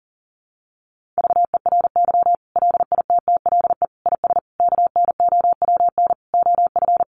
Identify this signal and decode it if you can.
Morse code, heard as a keyed tone, about 40 wpm, 700 Hz. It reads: VERY LITTLE IS KNOWN OF
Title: Sentences from Top 1500 Words 40wpm